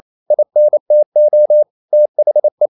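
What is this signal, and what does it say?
Morse code, heard as a keyed tone, about 28 wpm, 600 Hz. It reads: INTO THE
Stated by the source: Morse code practice at 28 wpm, text of Ken Follett's novel Never in English, from CW med Gnister.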